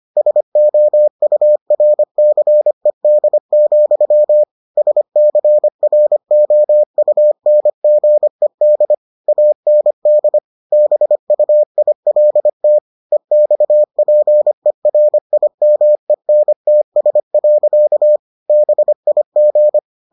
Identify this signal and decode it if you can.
Morse code, heard as a keyed tone, about 25 wpm, 600 Hz. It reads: SOURCED, SCROUNGED AND BUILT EXPERIMENTS. BIG